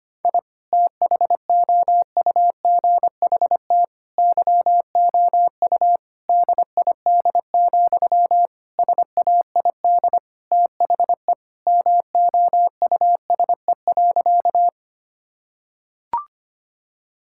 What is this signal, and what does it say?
Morse code, heard as a keyed tone, about 25 wpm, 700 Hz. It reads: I THOUGHT YOU DID, SAID THE MOUSE.
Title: Sentences from Top 1000 Words - Review Part 5 - 25wpm